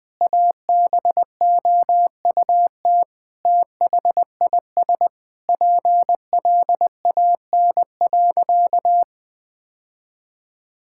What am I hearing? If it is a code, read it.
Morse code, heard as a keyed tone, about 21 wpm, 700 Hz. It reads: ABOUT THIS PLAN.
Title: Sentences with 9-Word Phrases - 20wpm